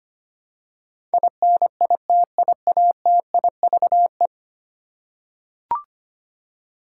Morse code, heard as a keyed tone, about 25 wpm, 700 Hz. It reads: INITIATIVE
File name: Ten Letter Words - 25wpm